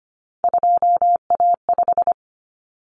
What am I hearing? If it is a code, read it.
Morse code, heard as a keyed tone, about 25 wpm, 700 Hz. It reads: 2A5
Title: Common Cut Numbers - 3-Digit 25wpm